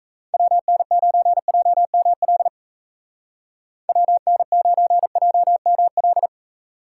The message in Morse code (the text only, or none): WN9JML WN9JML